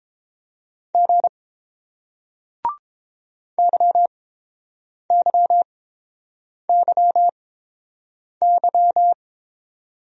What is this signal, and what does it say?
Morse code, heard as a keyed tone, about 27 wpm, spaced as for 7 wpm, 700 Hz. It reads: G YYYY
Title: Speed-Racing Single Letters 22wpm